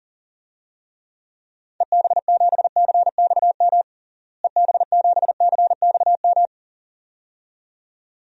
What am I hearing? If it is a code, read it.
Morse code, heard as a keyed tone, about 40 wpm, 700 Hz. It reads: EB7CXM EB7CXM
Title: DXCC-100 Call Signs - 6 Characters - Repeated 2x - 40wpm